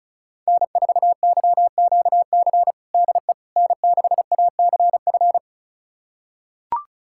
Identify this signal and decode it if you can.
Morse code, heard as a keyed tone, about 36 wpm, 700 Hz. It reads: N4YQC DE N6ACF
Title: USA Callsign DE Callsign - 5 Characters - 35wpm